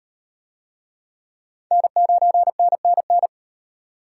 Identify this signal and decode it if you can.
Morse code, heard as a keyed tone, about 38 wpm, 700 Hz. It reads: N9NNN